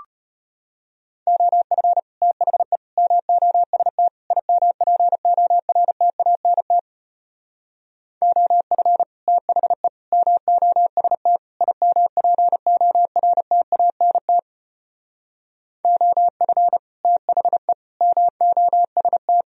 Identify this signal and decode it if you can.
Morse code, heard as a keyed tone, about 33 wpm, 700 Hz. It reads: OF THE MOST IMPORTANT OF THE MOST IMPORTANT OF THE MOST